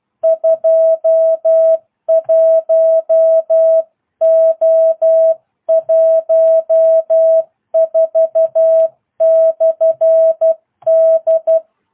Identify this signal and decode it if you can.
Morse code, heard as a keyed tone, about 12 wpm, 650 Hz. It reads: 21O14/D